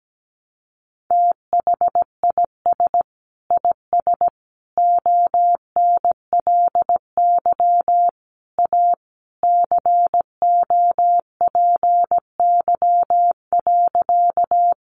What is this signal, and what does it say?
Morse code, heard as a keyed tone, about 17 wpm, 700 Hz. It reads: THIS IS ONLY A COPY.
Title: Sentences from Top 1400 Words 17wpm